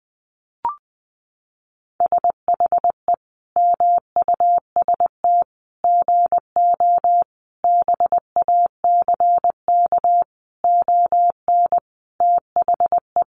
Morse code, heard as a keyed tone, about 20 wpm, 700 Hz. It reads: SHE MUST GO BACK ON THE
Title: Sentences from Top 1200 Words 20wpm